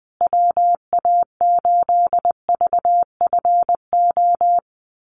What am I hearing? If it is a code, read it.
Morse code, heard as a keyed tone, about 20 wpm, 700 Hz. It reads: WA8VFO
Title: USA Call Signs Repeated 3x 20wpm